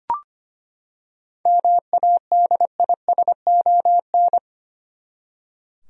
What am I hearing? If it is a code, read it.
Morse code, heard as a keyed tone, about 25 wpm, 700 Hz. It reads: MADISON
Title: USA Common Names - 25wpm